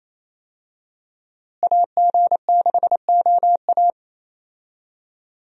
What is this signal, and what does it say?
Morse code, heard as a keyed tone, about 28 wpm, 700 Hz. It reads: AG6OA
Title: USA Call Signs - 5 Characters - Repeated 2x - 28wpm